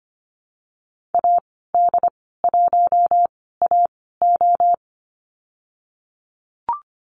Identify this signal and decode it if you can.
Morse code, heard as a keyed tone, about 25 wpm, spaced as for 16 wpm, 700 Hz. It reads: AD1AO
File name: Lesson 15g - Number 5 - Call Signs 25wpm